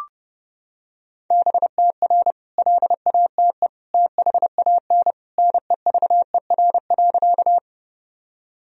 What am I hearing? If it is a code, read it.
Morse code, heard as a keyed tone, about 30 wpm, 700 Hz. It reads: BTR LATE THAN NEVER.
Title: Sentences from Top 400 Words and Abbreviations 30wpm